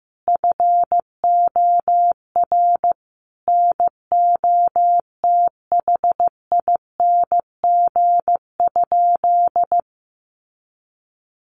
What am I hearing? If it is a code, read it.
Morse code, heard as a keyed tone, about 15 wpm, 700 Hz. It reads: FOR NOTHING?
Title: Sentences from Top 500 Words 15wpm